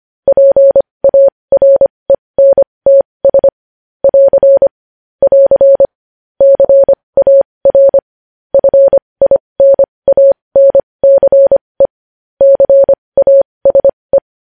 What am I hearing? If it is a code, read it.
Morse code, heard as a keyed tone, about 25 wpm, spaced as for 20 wpm, 550 Hz. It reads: PARENTS <AR> <AR> CAR FINANCE CASE
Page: news-headlines-in-morse-code-at-20-wpm